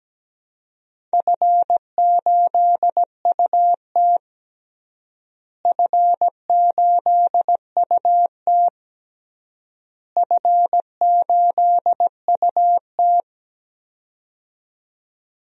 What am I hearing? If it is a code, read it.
Morse code, heard as a keyed tone, about 17 wpm, 700 Hz. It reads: F8UT F8UT F8UT
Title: DXCC-100 Call Signs - 4 Characters - Repeated 3x - 17wpm